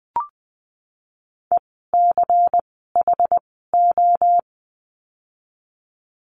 Morse code, heard as a keyed tone, about 20 wpm, spaced as for 15 wpm, 700 Hz. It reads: ECHO
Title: 094 Lesson 16f - Letter c - Words 20wpm